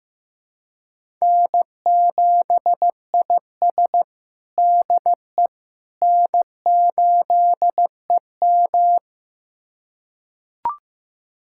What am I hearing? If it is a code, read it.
Morse code, heard as a keyed tone, about 15 wpm, 700 Hz. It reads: N7IS DE N8EM
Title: USA Callsign DE Callsign - 4 Characters - 15wpm